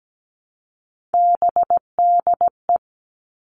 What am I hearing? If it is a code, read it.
Morse code, heard as a keyed tone, about 17 wpm, 700 Hz. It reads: BDE
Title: Three Challenging Letter-Number Combinations 17wpm